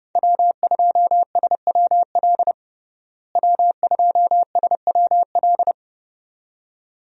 Morse code, heard as a keyed tone, about 30 wpm, 700 Hz. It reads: W2SWL W2SWL
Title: USA Call Signs - 5 Characters - Repeated 2x - 30wpm